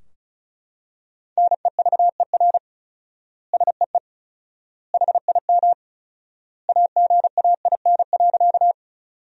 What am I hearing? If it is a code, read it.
Morse code, heard as a keyed tone, about 36 wpm, 700 Hz. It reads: NEVER SEE HIM AGAIN.
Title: 4x Word-Spacing - Sentences from Top 300 Words - 35wpm